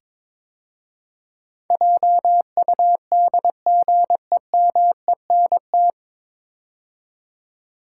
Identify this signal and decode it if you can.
Morse code, heard as a keyed tone, about 22 wpm, 700 Hz. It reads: JUDGEMENT
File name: 9 Letter Words - First 3 Letters Given - 22wpm